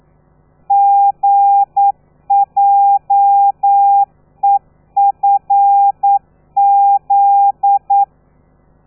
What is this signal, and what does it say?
Morse code, heard as a keyed tone, about 9 wpm, 800 Hz. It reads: GJEFZ
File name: telegrafia / i2tfj / Lez13 40